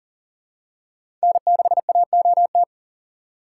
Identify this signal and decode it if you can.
Morse code, heard as a keyed tone, about 40 wpm, 700 Hz. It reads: N6AOT
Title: USA Call Signs - 5 Characters - 2x Character Spacing - 40wpm